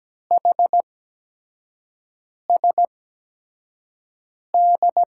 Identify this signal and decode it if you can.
Morse code, heard as a keyed tone, about 17 wpm, spaced as for 5 wpm, 700 Hz. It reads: HSD
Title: USA Call Signs - 8x Character Spacing - 17wpm